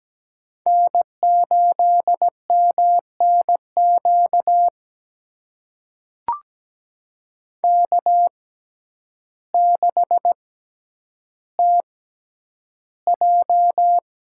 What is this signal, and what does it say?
Morse code, heard as a keyed tone, about 17 wpm, spaced as for 9 wpm, 700 Hz. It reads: N8MNQ K6TJ
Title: USA Call Signs - 6x Character Spacing - 17wpm